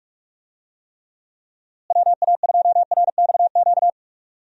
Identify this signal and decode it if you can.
Morse code, heard as a keyed tone, about 46 wpm, 700 Hz. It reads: WA2RXQ
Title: USA Call Signs - 6 Characters - Repeated 2x - 45wpm